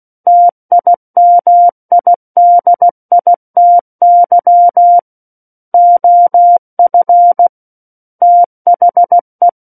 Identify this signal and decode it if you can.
Morse code, heard as a keyed tone, about 16 wpm, 700 Hz. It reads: TIMIDITY OF THE
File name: Book One / 07-Chapter 7 - 16 WPM